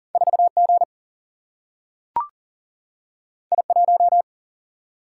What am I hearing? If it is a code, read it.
Morse code, heard as a keyed tone, about 40 wpm, 700 Hz. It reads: 4G I1
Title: Single-Letter Single-Number 40wpm